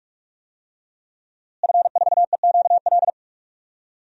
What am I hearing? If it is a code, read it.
Morse code, heard as a keyed tone, about 46 wpm, 700 Hz. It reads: F4EQL
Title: DXCC-100 Call Signs - 5 Characters - Repeated 3x - 45wpm